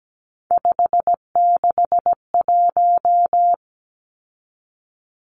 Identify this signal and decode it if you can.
Morse code, heard as a keyed tone, about 17 wpm, 700 Hz. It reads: 561
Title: Numbers - 3-Digit 17wpm